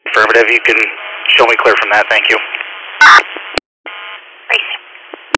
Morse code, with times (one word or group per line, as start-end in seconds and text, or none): none